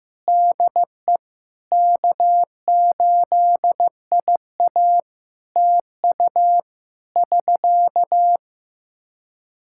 Morse code, heard as a keyed tone, about 15 wpm, 700 Hz. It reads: DE K8IA TU <SK>
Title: K1USN SST QSO Practice 15wpm